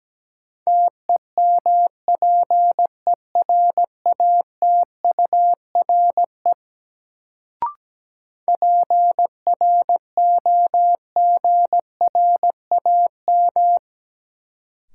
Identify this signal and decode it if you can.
Morse code, heard as a keyed tone, about 17 wpm, 700 Hz. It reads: TEMPERATURE PROGRAM
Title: Sentences from Top 1000 Words - Review Part 4 - New Word Drill - 17wpm